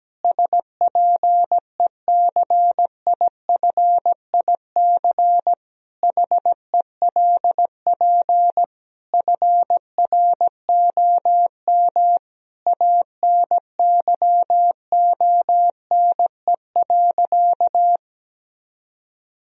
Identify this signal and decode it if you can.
Morse code, heard as a keyed tone, about 17 wpm, 700 Hz. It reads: SPECIFIC HELP FROM ANYONE.
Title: Sentences from Top 1000 Words - Review Part 3 - 17wpm